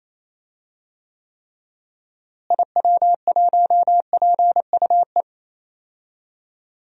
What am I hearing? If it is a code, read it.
Morse code, heard as a keyed tone, about 28 wpm, 700 Hz. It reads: IW1PUE
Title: DXCC-100 Call Signs - 6 Characters - Repeated 3x - 28wpm